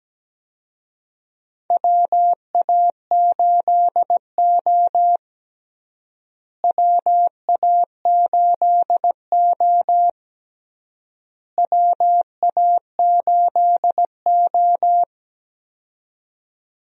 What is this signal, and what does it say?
Morse code, heard as a keyed tone, about 17 wpm, 700 Hz. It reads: WA8O WA8O WA8O
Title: USA Call Signs - 4 Characters - Repeated 3x - 17wpm